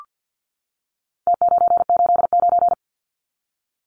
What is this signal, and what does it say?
Morse code, heard as a keyed tone, about 50 wpm, 700 Hz. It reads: T989
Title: Numbers - Contest Style - 4-Digit 50wpm